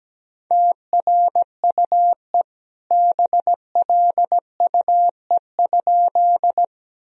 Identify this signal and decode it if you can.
Morse code, heard as a keyed tone, about 17 wpm, 700 Hz. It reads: TRUE BLUE?
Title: Sentences from Top 1000 Words - Review Part 3 - 17wpm